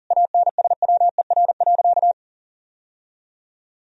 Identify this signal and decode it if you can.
Morse code, heard as a keyed tone, about 37 wpm, 700 Hz. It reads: ANSWER.
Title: Sentences from Top 1000 Words - Review Part 4 - 40wpm